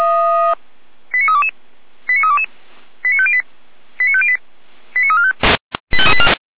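Five-tone selective call (ZVEI), heard as quick runs of five tones, tones of about 70 ms. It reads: 89322 89322 89508 89508 89345
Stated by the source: ZVEI five-tone selective call received by SDR, from the Feuerwehr Orlamünde (fire brigade) audio server.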